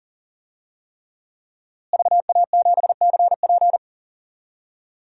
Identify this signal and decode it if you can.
Morse code, heard as a keyed tone, about 40 wpm, 700 Hz. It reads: VA7CP
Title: DXCC-100 Call Signs - 5 Characters - Repeated 3x - 40wpm